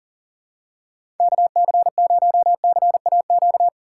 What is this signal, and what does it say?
Morse code, heard as a keyed tone, about 40 wpm, 700 Hz. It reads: KC0CAQ
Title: USA Call Signs - 6 Characters - 6x Character Spacing - 40wpm